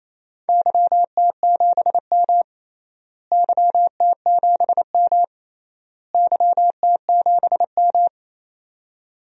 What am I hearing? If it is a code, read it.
Morse code, heard as a keyed tone, about 28 wpm, 700 Hz. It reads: YT7M YT7M YT7M
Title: DXCC-100 Call Signs - 4 Characters - Repeated 3x - 28wpm